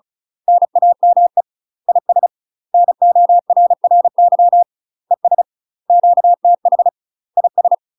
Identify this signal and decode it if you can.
Morse code, heard as a keyed tone, about 36 wpm, 700 Hz. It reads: NAME IS NORRY ES QTH IS
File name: UK MASTER QSOs TRAINING / 35WPM / UK MASTER QSOs 0041